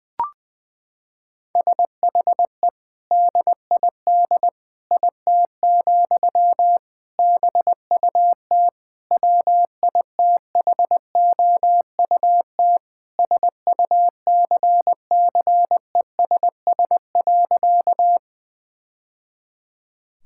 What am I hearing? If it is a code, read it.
Morse code, heard as a keyed tone, about 20 wpm, 700 Hz. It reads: SHE DID IT, BUT WITHOUT SUCCESS.
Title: Sentences from Top 1000 Words - Review Part 6 - 20wpm